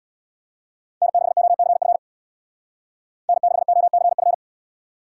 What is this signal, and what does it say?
Morse code, heard as a keyed tone, about 61 wpm, 700 Hz. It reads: N6OCF N6OCF
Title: Speed Racing - USA 1x3 Call Signs 50wpm